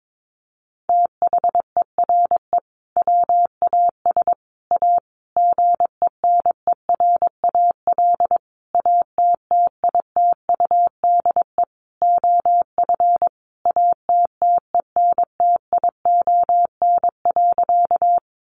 Morse code, heard as a keyed tone, about 22 wpm, 700 Hz. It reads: THERE WAS A GENERAL ATTITUDE OF ATTENTION.
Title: Sentences from Top 1300 Words 22wpm